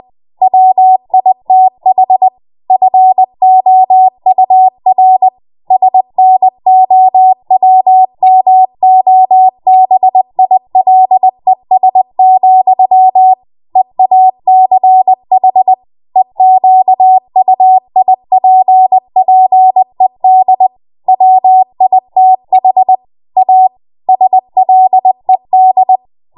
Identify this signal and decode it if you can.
Morse code, heard as a keyed tone, about 20 wpm, 750 Hz. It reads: WITH FOUR SNOWMOBILES, EACH EQUIPPED WITH A SLED
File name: Morse / 20WPM